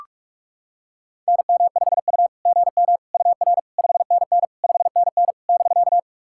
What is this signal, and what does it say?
Morse code, heard as a keyed tone, about 44 wpm, 660 Hz. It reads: NM5U GM UR 5NN 5NN <BK>
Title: SOTA QSO Practice 45wpm